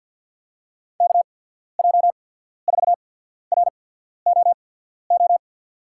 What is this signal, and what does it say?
Morse code, heard as a keyed tone, about 53 wpm, spaced as for 14 wpm, 700 Hz. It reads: KJ4ROO
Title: USA Call Signs - 6 Characters - 8x Character Spacing - 50wpm